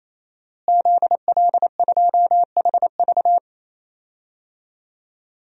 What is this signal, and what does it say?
Morse code, heard as a keyed tone, about 28 wpm, 700 Hz. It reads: ZL2HV
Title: DXCC-100 Call Signs - 5 Characters - Repeated 3x - 28wpm